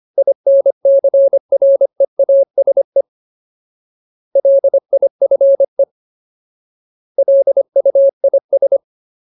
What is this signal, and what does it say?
Morse code, heard as a keyed tone, about 25 wpm, 550 Hz. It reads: INCREASE LIFE LUIS
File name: Ultimate Word List / 25WPM Ultimate